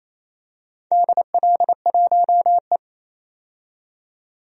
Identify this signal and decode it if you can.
Morse code, heard as a keyed tone, about 29 wpm, 700 Hz. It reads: DL1E
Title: DXCC-100 Call Signs - 4 Characters - 28wpm